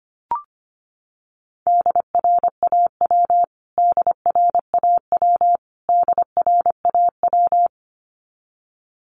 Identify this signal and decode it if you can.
Morse code, heard as a keyed tone, about 25 wpm, 700 Hz. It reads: DRAW DRAW DRAW
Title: Top 500 Words Repeated 25wpm